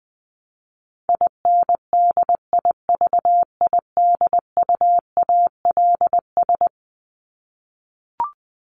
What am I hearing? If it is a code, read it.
Morse code, heard as a keyed tone, about 20 wpm, 700 Hz. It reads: INDIVIDUALS
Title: ES:S Suffix - Top 100 Words with Base Words 20wpm